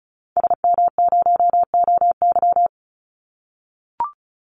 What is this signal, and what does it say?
Morse code, heard as a keyed tone, about 34 wpm, 700 Hz. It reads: SM0OY
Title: Call Signs - 35wpm